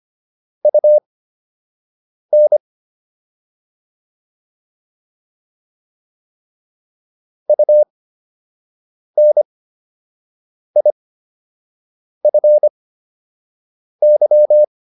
Morse code, heard as a keyed tone, about 25 wpm, spaced as for 6 wpm, 600 Hz. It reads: UN UNIFY